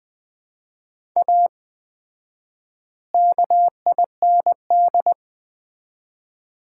Morse code, heard as a keyed tone, about 20 wpm, 700 Hz. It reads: A KIND
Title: 4x Word-Spacing - Sets of 3 Words - 20wpm